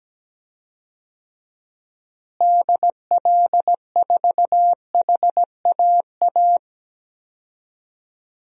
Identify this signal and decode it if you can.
Morse code, heard as a keyed tone, about 17 wpm, 700 Hz. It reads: DL4HAA
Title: DXCC-100 Call Signs - 6 Characters - Repeated 3x - 17wpm